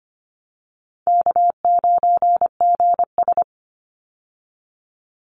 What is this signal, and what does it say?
Morse code, heard as a keyed tone, about 25 wpm, 700 Hz. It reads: K9GS
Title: Easy Call Signs Repeated 3x - 25wpm